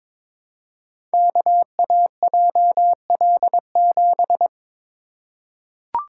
Morse code, heard as a keyed tone, about 22 wpm, 700 Hz. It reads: KAJL7
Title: Code Groups - 5 Letter:Numbers - 22wpm